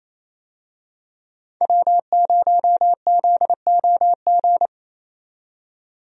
Morse code, heard as a keyed tone, about 28 wpm, 700 Hz. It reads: W0ZOG
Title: USA Call Signs - 5 Characters - 28wpm